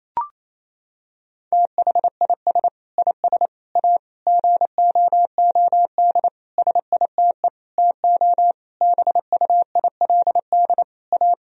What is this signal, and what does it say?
Morse code, heard as a keyed tone, about 28 wpm, 700 Hz. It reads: THIS IS A GOOD SITE TO BUILD A
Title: Sentences from Top 1000 Words - Review Part 4 - 28wpm